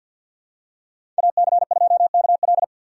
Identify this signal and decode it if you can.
Morse code, heard as a keyed tone, about 52 wpm, 690 Hz. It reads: AC2KL
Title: USA Call Signs - 5 Characters - 3x Character Spacing - 50wpm